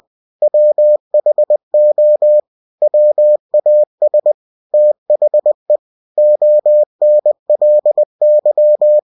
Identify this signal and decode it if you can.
Morse code, heard as a keyed tone, about 20 wpm, 600 Hz. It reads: WHO WAS THE ONLY
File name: cw-books / winnie-the-pooh / winnie-20wpm-page-0029